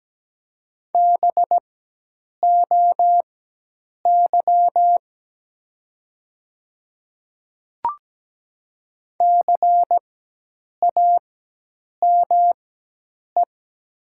Morse code, heard as a keyed tone, about 17 wpm, spaced as for 8 wpm, 700 Hz. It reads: BOY CAME
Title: 4x Character Spacing - Top 200 Words - 17wpm